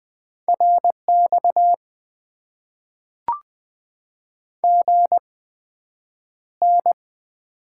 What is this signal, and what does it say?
Morse code, heard as a keyed tone, about 20 wpm, spaced as for 9 wpm, 700 Hz. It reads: RX GN
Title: Top 90 QSO Elements - 8x Character Spacing - 20wpm